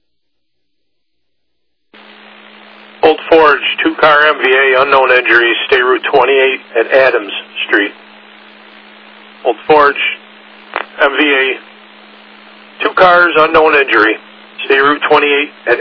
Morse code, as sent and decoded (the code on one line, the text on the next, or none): none